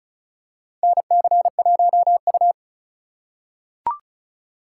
Morse code, heard as a keyed tone, about 36 wpm, 700 Hz. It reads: NC1U
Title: USA Call Signs - 4 Characters - 3x Character Spacing - 35wpm